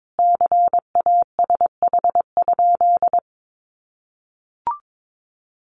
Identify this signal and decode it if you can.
Morse code, heard as a keyed tone, about 22 wpm, 700 Hz. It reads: CASH?
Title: Sentences from Top 700 Words 22wpm